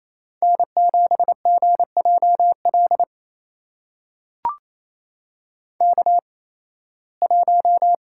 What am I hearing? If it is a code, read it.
Morse code, heard as a keyed tone, about 28 wpm, spaced as for 18 wpm, 700 Hz. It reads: N7GJL K1
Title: USA Call Signs - 5 Characters - 8x Character Spacing - 28wpm